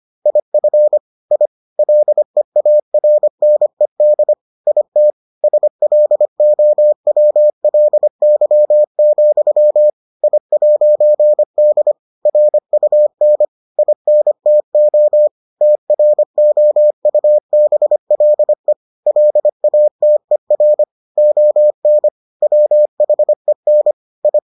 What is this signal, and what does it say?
Morse code, heard as a keyed tone, about 25 wpm, 600 Hz. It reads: IF I LEARNED IT SLOWLY, I'D RUN INTO TROUBLE LATER ON WHEN I